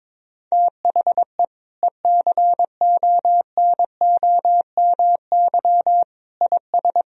Morse code, heard as a keyed tone, about 22 wpm, 700 Hz. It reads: THE ECONOMY IS
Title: Sentences from Top 1000 Words - Review Part 3 - 22wpm